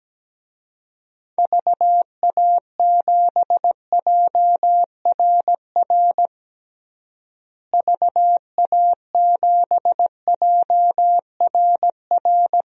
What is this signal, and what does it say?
Morse code, heard as a keyed tone, about 17 wpm, 700 Hz. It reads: VA7JRR VA7JRR
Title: DXCC-100 Call Signs - 6 Characters - Repeated 3x - 17wpm